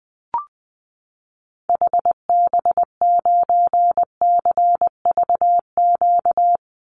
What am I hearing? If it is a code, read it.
Morse code, heard as a keyed tone, about 20 wpm, 700 Hz. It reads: HB9CVQ
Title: Call Signs Repeated 2x - 20wpm